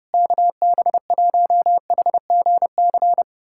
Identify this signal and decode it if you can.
Morse code, heard as a keyed tone, about 30 wpm, 700 Hz. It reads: KB1HGC
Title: USA Callsign DE Callsign 30wpm